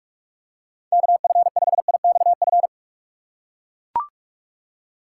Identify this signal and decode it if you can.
Morse code, heard as a keyed tone, about 44 wpm, 700 Hz. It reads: KF5IXF
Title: USA Call Signs - 6 Characters - 6x Character Spacing - 45wpm